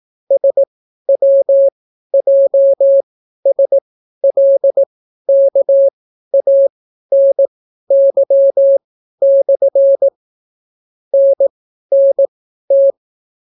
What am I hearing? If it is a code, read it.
Morse code, heard as a keyed tone, about 18 wpm, spaced as for 12 wpm, 550 Hz. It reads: SWJSLKANY/ NNT